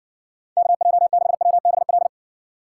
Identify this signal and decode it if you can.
Morse code, heard as a keyed tone, about 60 wpm, 700 Hz. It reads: KJ6WBL